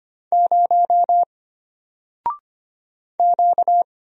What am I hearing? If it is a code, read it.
Morse code, heard as a keyed tone, about 25 wpm, 700 Hz. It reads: 0 Q
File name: Single-Letter-Number-25wpm